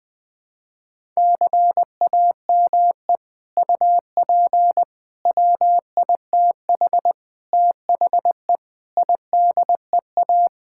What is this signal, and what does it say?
Morse code, heard as a keyed tone, about 20 wpm, 700 Hz. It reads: CAME UP WITH THE IDEA
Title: Sets of 6 Words - 20wpm